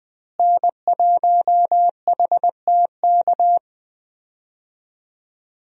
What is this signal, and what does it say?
Morse code, heard as a keyed tone, about 20 wpm, 700 Hz. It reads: N1HTK
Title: USA Call Signs - 5 Characters - Repeated 3x - 20wpm